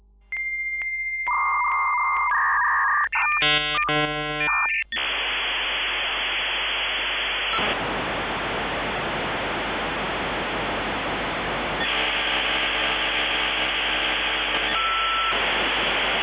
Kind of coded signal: dial-up modem handshake